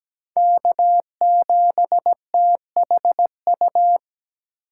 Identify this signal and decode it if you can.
Morse code, heard as a keyed tone, about 17 wpm, 700 Hz. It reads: K7THU
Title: USA Call Signs - 5 Characters - 3x Character Spacing - 17wpm